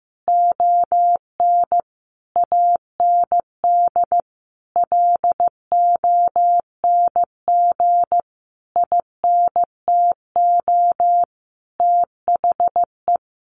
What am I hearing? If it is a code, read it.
Morse code, heard as a keyed tone, about 15 wpm, 700 Hz. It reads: ON AND LONG INTO THE
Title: Sentences from Top 1500 Words 15wpm